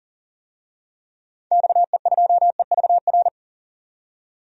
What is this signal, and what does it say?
Morse code, heard as a keyed tone, about 40 wpm, 700 Hz. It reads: XE2EVR
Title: DXCC-100 Call Signs - 6 Characters - 40wpm